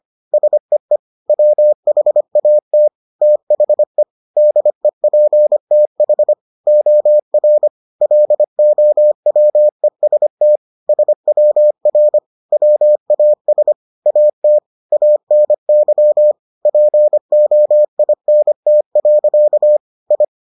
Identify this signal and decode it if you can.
Morse code, heard as a keyed tone, about 25 wpm, 600 Hz. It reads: SEE WHAT THE DEPTH OR LOWEST SWR WAS AT ANY POINT. I